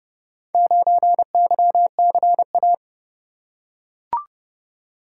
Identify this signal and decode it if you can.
Morse code, heard as a keyed tone, about 30 wpm, 700 Hz. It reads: 9YCA